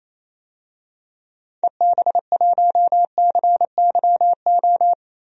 Morse code, heard as a keyed tone, about 28 wpm, 700 Hz. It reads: EB1CYO
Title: DXCC-100 Call Signs - 6 Characters - 28wpm